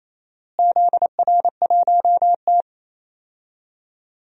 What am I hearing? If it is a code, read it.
Morse code, heard as a keyed tone, about 28 wpm, 700 Hz. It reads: ZR1T
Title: DXCC-100 Call Signs - 4 Characters - Repeated 3x - 28wpm